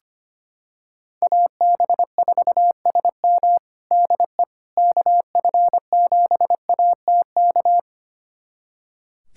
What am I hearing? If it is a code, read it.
Morse code, heard as a keyed tone, about 25 wpm, 700 Hz. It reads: AB4SM DE KF7ATK
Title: USA Callsign DE Callsign 25wpm